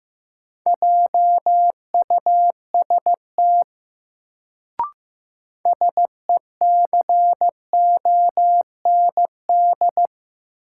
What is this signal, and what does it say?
Morse code, heard as a keyed tone, about 15 wpm, 700 Hz. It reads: JUST SECOND
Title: Sets of 2 Words - New Word Drill - 15wpm